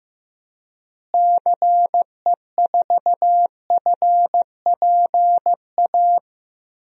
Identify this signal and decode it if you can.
Morse code, heard as a keyed tone, about 16 wpm, 700 Hz. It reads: CE4FPA
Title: DXCC-100 Call Signs - 6 Characters - Repeated 2x - 15wpm